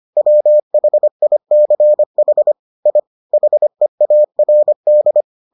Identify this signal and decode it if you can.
Morse code, heard as a keyed tone, about 25 wpm, 600 Hz. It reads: WHICH I HEARD